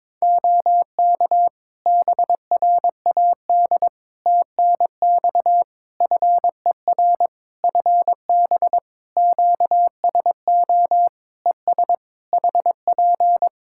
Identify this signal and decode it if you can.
Morse code, heard as a keyed tone, about 22 wpm, 700 Hz. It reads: OK BRAD TNX FER FB QSO ES HP
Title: K1USN SST QSO Practice 22wpm